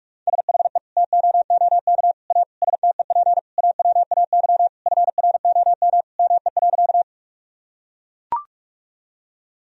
Morse code, heard as a keyed tone, about 41 wpm, 700 Hz. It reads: SHE TOOK A STEP AWAY FROM ME.
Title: Sentences from Top 1000 Words - Review Part 2 - 45wpm